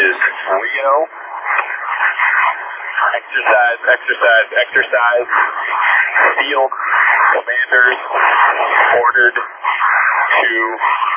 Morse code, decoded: none

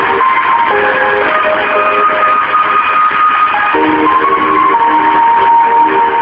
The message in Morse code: none

